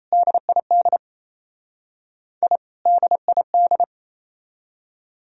Morse code, heard as a keyed tone, about 35 wpm, 700 Hz. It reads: DID I DID
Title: Speed-Racing - Sets of 2 Words 28wpm